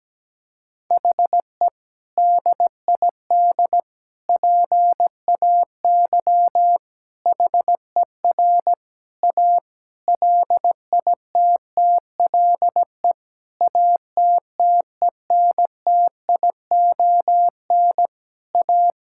Morse code, heard as a keyed tone, about 17 wpm, 700 Hz. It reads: HE DID PAY HER A LITTLE ATTENTION A
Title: Sentences from Top 1000 Words - Review Part 6 - 17wpm